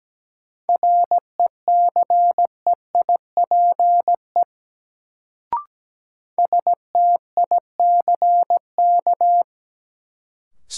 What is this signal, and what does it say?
Morse code, heard as a keyed tone, about 17 wpm, 700 Hz. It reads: RECEIPE STICK
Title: Sentences from Top 1200 Words - New Word Drill - 17wpm